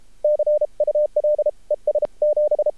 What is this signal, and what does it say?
Morse code, heard as a keyed tone, about 53 wpm, spaced as for 40 wpm, 610 Hz. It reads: CUL ES 7
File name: archived / CRSnet